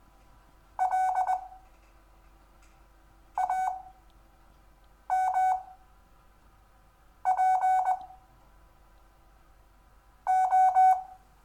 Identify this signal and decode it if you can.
Morse code, heard as a keyed tone, about 20 wpm, spaced as for 5 wpm, 750 Hz. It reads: L AMP O
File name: cw / Lesson6 WI